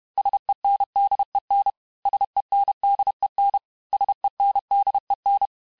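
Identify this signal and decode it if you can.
Morse code, heard as a keyed tone, about 31 wpm, 800 Hz. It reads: SENDEN SENDEN SENDEN
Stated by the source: Morse lesson of words, each word repeated 4 times, at 29 words per minute.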